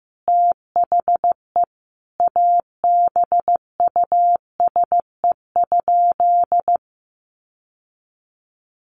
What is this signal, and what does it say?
Morse code, heard as a keyed tone, about 15 wpm, 700 Hz. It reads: THE ABUSE?
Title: Sentences from Top 1300 Words 15wpm